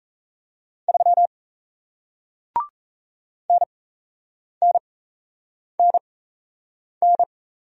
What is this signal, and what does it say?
Morse code, heard as a keyed tone, about 38 wpm, spaced as for 8 wpm, 700 Hz. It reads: 3 NNNN